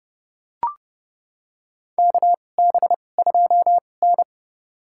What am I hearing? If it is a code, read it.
Morse code, heard as a keyed tone, about 30 wpm, spaced as for 22 wpm, 700 Hz. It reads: KB2N
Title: USA Call Signs - 4 Characters - 2x Character Spacing - 30wpm